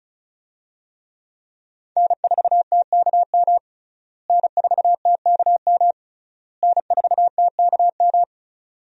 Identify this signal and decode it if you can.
Morse code, heard as a keyed tone, about 34 wpm, 700 Hz. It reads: N4TKM N4TKM N4TKM
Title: USA Call Signs - 5 Characters - Repeated 3x - 35wpm